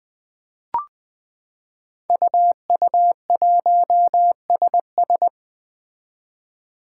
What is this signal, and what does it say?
Morse code, heard as a keyed tone, about 20 wpm, 700 Hz. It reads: UU1SS
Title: Five Challenging Letter-Number Combinations 20wpm